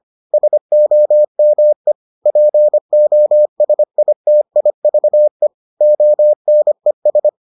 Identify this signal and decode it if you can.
Morse code, heard as a keyed tone, about 25 wpm, 600 Hz. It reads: SOME POSITIVE ONES